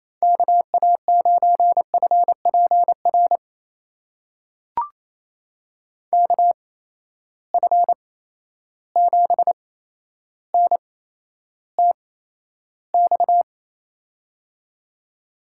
Morse code, heard as a keyed tone, about 27 wpm, spaced as for 12 wpm, 700 Hz. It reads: KA9FPR KF7NTX
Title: USA Call Signs - 6 Characters - 8x Character Spacing - 28wpm